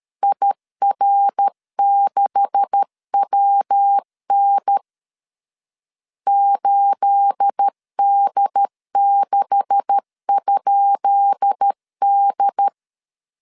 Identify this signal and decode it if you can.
Morse code, heard as a keyed tone, about 13 wpm, 800 Hz. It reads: IR6WN 8D6?D
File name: LEZIONI / 10 SIGNS / 10B INT 12wpm 2414